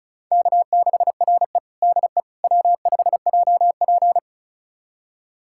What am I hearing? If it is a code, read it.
Morse code, heard as a keyed tone, about 36 wpm, 700 Hz. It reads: K6RE DE W5JP
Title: USA Callsign DE Callsign - 4 Characters - 35wpm